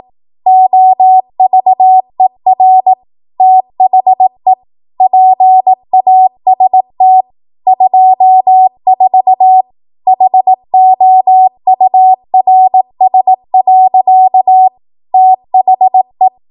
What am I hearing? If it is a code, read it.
Morse code, heard as a keyed tone, about 18 wpm, 750 Hz. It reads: OVER THE PAST 24 HOURS. THE
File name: Morse / 18WPM